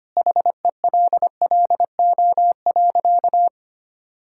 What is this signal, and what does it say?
Morse code, heard as a keyed tone, about 25 wpm, 700 Hz. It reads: HELLO.
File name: practice / 25wpm / Sentences with 2-Word Phrases - 25wpm